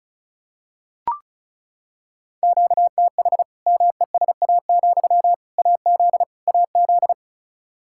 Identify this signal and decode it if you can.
Morse code, heard as a keyed tone, about 36 wpm, 700 Hz. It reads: QTH MESA, AZ AZ
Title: Practice QSO - Exchange 1 - Beginning - 35wpm